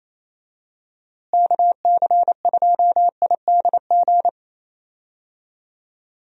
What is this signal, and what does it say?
Morse code, heard as a keyed tone, about 28 wpm, 700 Hz. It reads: KC2IDG
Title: USA Call Signs - 6 Characters - Repeated 2x - 28wpm